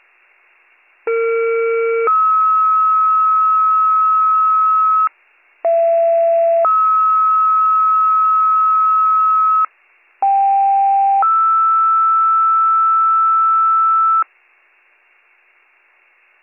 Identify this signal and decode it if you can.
Two-tone page, heard as a one-second tone, then a three-second tone, about 460 then 1290 Hz, about 670 then 1290 Hz, about 790 then 1320 Hz. It is heard from 1.1 to 14.2 s.